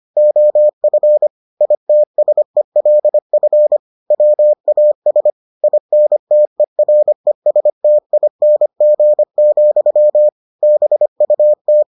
Morse code, heard as a keyed tone, about 25 wpm, 600 Hz. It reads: OF ITSELF WAS INTERESTING, BUT